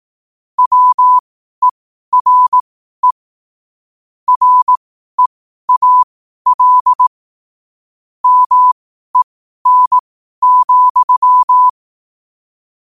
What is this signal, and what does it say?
Morse code, heard as a keyed tone, about 18 wpm, spaced as for 12 wpm, 1000 Hz. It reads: WERE REAL MEN,
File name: qotdfiles / 12wpm